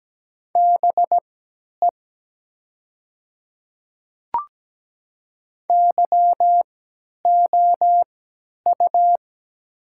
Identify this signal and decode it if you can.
Morse code, heard as a keyed tone, about 17 wpm, spaced as for 10 wpm, 700 Hz. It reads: BE YOU
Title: 3x Character Spacing - Top 200 Words - 17wpm